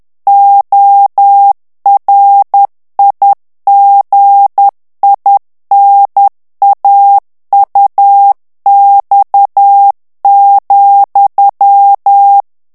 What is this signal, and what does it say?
Morse code, heard as a keyed tone, about 10 wpm, 800 Hz. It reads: ORIGINAUX,
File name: CW / FICHIERS-MP3 / FRANCAIS TEXTE CLAIR 2 VITESSE 10 MOTS MINUTES